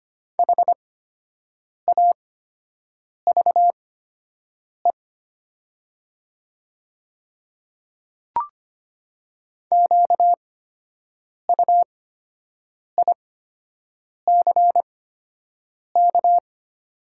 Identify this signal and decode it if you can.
Morse code, heard as a keyed tone, about 25 wpm, spaced as for 7 wpm, 700 Hz. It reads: HAVE QUICK
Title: 8x Character Spacing - Top 500 Words - 25wpm